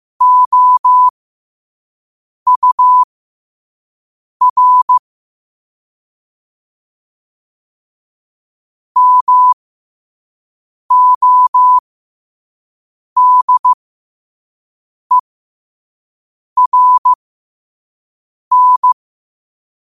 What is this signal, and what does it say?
Morse code, heard as a keyed tone, about 15 wpm, spaced as for 5 wpm, 1000 Hz. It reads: OUR MODERN
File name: qotdfiles / 5wpm